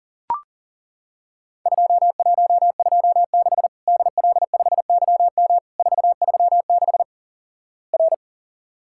Morse code, heard as a keyed tone, about 40 wpm, 700 Hz. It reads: 2126 DL5YM 436 R
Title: Worked All Europe DX Contest - QTC Exchange - 40wpm